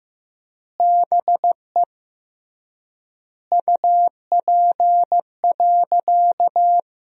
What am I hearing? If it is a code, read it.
Morse code, heard as a keyed tone, about 15 wpm, 700 Hz. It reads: BE UP.
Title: 3x Word-Spacing - Sentences from Top 300 Words - 15wpm